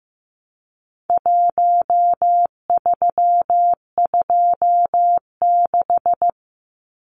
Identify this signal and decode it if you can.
Morse code, heard as a keyed tone, about 15 wpm, 700 Hz. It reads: 1326
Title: Numbers - 4-Digit 15wpm